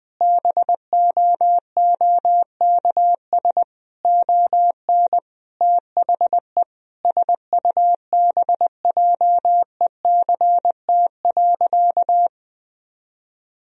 Morse code, heard as a keyed tone, about 20 wpm, 700 Hz. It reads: BOOKS ON THE SUBJECT.
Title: Sentences from Top 1000 Words - Review Part 4 - 20wpm